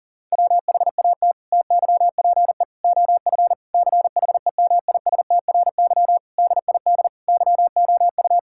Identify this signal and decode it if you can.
Morse code, heard as a keyed tone, about 40 wpm, 700 Hz. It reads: WHAT TYPE OF CHEMISTRY DID YOU
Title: Sentences from Top 1000 Words - Review Part 6 - 40wpm